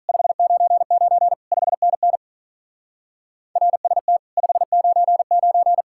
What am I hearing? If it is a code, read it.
Morse code, heard as a keyed tone, about 44 wpm, 700 Hz. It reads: 599 5NN RST 599